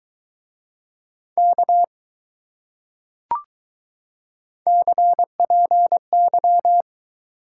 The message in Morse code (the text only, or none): K CPY